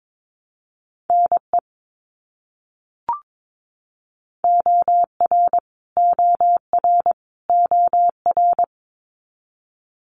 Morse code, heard as a keyed tone, about 22 wpm, 700 Hz. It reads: NE OR OR OR
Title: US States Repeated 3x 22wpm